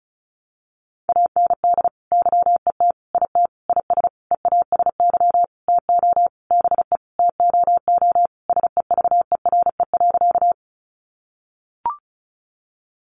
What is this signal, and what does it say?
Morse code, heard as a keyed tone, about 34 wpm, 700 Hz. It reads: AND YET IT IS EASY TO BE TOO SEVERE.
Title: Sentences from Top 1400 Words 35wpm